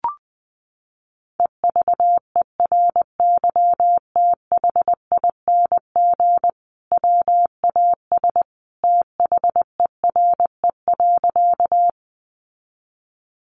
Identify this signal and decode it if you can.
Morse code, heard as a keyed tone, about 20 wpm, 700 Hz. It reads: EVERYTHING WAS THERE.
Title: Sentences from Top 700 Words 20wpm